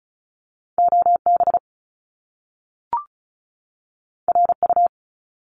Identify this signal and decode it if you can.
Morse code, heard as a keyed tone, about 36 wpm, 700 Hz. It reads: OB RU
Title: Top Two Letter Combinations in Words 35wpm